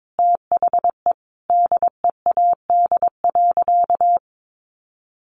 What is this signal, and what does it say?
Morse code, heard as a keyed tone, about 22 wpm, 700 Hz. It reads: THE DEAD.
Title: Sentences from Top 1500 Words 22wpm